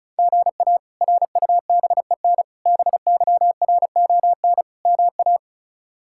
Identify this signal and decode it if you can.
Morse code, heard as a keyed tone, about 36 wpm, 700 Hz. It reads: GA RUBEN BYRON MA
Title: K1USN SST QSO Practice - 35wpm